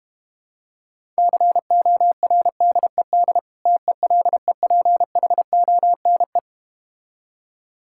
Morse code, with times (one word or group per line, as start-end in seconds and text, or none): 1.2-3.4 CORDED
3.7-6.4 TELEPHONE